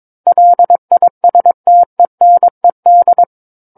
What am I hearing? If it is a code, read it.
Morse code, heard as a keyed tone, about 22 wpm, 700 Hz. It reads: LISTENED
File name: Book One / 13-Chapter 13 - 22 WPM